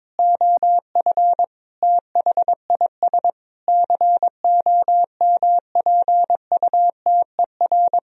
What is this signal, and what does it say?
Morse code, heard as a keyed tone, about 22 wpm, 700 Hz. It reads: OF THIS COMPUTER
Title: Sentences from Top 1000 Words - Review Part 3 - 22wpm